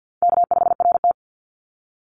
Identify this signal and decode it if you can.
Morse code, heard as a keyed tone, about 50 wpm, 700 Hz. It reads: K5RT
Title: Call Signs - 50wpm